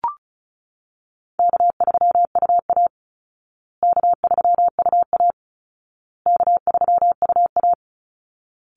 Morse code, heard as a keyed tone, about 34 wpm, 700 Hz. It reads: K3UA K3UA K3UA
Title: Call Signs Repeated 2x - 35wpm